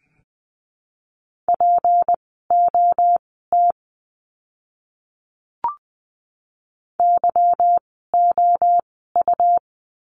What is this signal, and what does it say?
Morse code, heard as a keyed tone, about 20 wpm, spaced as for 14 wpm, 700 Hz. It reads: POT YOU
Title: 153 Lesson 25f - Letter p - Words 20wpm